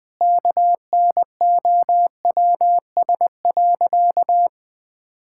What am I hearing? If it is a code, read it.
Morse code, heard as a keyed tone, about 20 wpm, 700 Hz. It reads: KNOWS.
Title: Sentences with 2-Word Phrases - Spoken Beginning - 20wpm